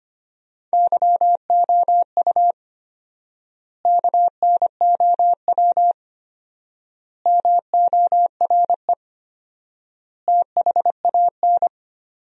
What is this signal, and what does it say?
Morse code, heard as a keyed tone, about 25 wpm, 700 Hz. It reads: YOU KNOW MORE THAN
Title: 4x Word-Spacing - Sentences from Top 100 Words - 25wpm